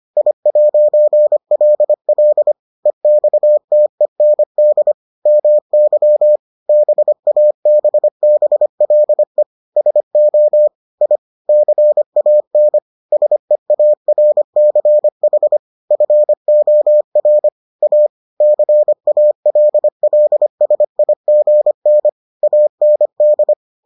Morse code, heard as a keyed tone, about 25 wpm, 600 Hz. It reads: I'LL EXTEND MY BABBLE SO I CAN SEARCH FOR A CALLSIGN AND